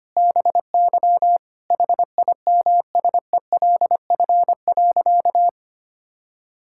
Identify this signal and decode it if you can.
Morse code, heard as a keyed tone, about 25 wpm, 700 Hz. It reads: BY HIMSELF.
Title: Sentences from Top 1600 Words 25wpm